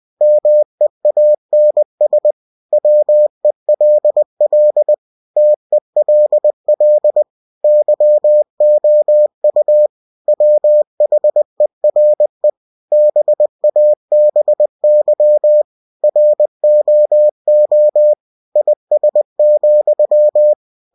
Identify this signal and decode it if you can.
Morse code, heard as a keyed tone, about 20 wpm, 600 Hz. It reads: MEANS WELL TELL YOU WHERE BABY ROO IS,